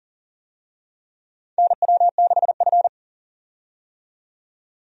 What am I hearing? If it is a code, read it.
Morse code, heard as a keyed tone, about 40 wpm, 700 Hz. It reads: NW6F
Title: USA Call Signs - 4 Characters - 40wpm